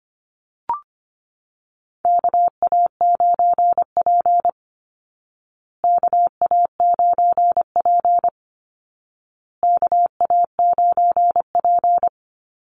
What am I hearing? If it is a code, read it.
Morse code, heard as a keyed tone, about 25 wpm, 700 Hz. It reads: KA9P KA9P KA9P
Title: Call Signs Repeated 2x - 25wpm